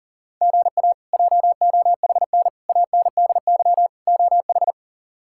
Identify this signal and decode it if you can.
Morse code, heard as a keyed tone, about 40 wpm, 690 Hz. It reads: GA JOHN ANDY OH
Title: K1USN SST QSO Practice - 40wpm